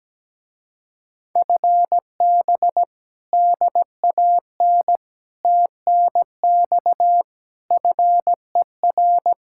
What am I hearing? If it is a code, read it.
Morse code, heard as a keyed tone, about 17 wpm, 700 Hz. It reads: FB DAN TNX FER
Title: Speed-Racing Common QSO Phrases - 17wpm